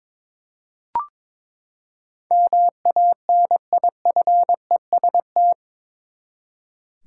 Morse code, heard as a keyed tone, about 21 wpm, 700 Hz. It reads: MANIFEST
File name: practice / 22wpm / Eight Letter Words - 22wpm